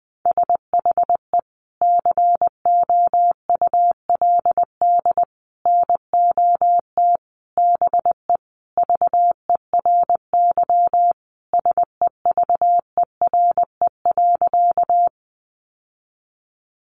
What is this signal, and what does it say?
Morse code, heard as a keyed tone, about 20 wpm, 700 Hz. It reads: SHE COULD NOT BE VERY SEVERE.
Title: Sentences from Top 1400 Words 20wpm